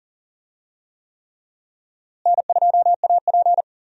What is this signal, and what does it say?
Morse code, heard as a keyed tone, about 40 wpm, 700 Hz. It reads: N2AP
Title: USA Call Signs - 4 Characters - Repeated 3x - 40wpm